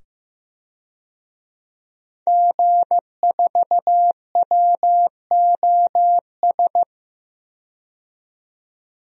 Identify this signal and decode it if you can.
Morse code, heard as a keyed tone, about 15 wpm, 700 Hz. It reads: G4WOS
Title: DXCC-100 Call Signs - 5 Characters - 15wpm